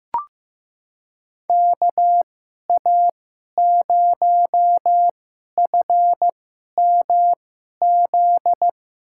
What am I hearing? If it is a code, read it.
Morse code, heard as a keyed tone, about 15 wpm, spaced as for 11 wpm, 700 Hz. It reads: KA0FMZ
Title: USA Call Signs - 2x Character Spacing - 15wpm